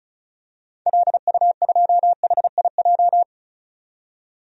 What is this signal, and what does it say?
Morse code, heard as a keyed tone, about 36 wpm, 700 Hz. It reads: LU2HIJ
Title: DXCC-100 Call Signs - 6 Characters - 35wpm